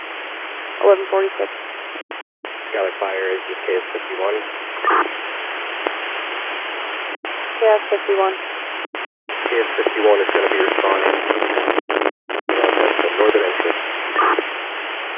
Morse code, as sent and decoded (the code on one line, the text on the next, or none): none